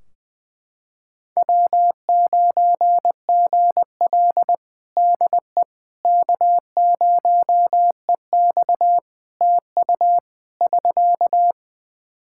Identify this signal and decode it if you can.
Morse code, heard as a keyed tone, about 20 wpm, 700 Hz. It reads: W9GL DE K0EX TU <SK>
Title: Practice QSO - Exchange 4 - Ending - 20wpm